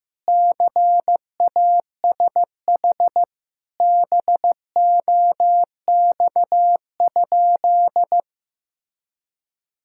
Morse code, heard as a keyed tone, about 15 wpm, 700 Hz. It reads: CASH BOX?
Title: Sentences from Top 1000 Words - Review Part 5 - 15wpm